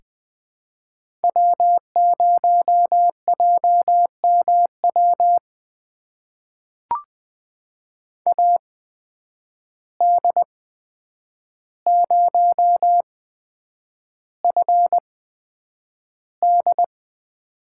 Morse code, heard as a keyed tone, about 20 wpm, spaced as for 9 wpm, 700 Hz. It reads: W0JMW AD0FD
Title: USA Call Signs - 5 Characters - 8x Character Spacing - 20wpm